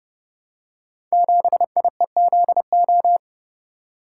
Morse code, heard as a keyed tone, about 30 wpm, 700 Hz. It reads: 7IEZO
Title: Code Groups - 5 Letter:Numbers - 30wpm